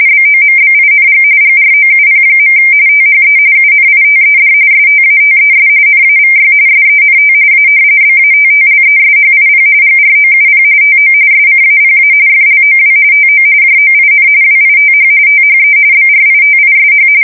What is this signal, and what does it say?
RTTY, about 45.45 baud. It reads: BOMBING GROUP. THE UNITED STATES NAVY SENT A BATTLESHIP GROUP TO SCAPA FLOW TO JOIN WITH THE BRITISH
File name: fsk170 / usnavy4